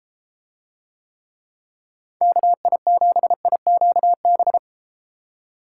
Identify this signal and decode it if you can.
Morse code, heard as a keyed tone, about 33 wpm, 700 Hz. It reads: KI7IQB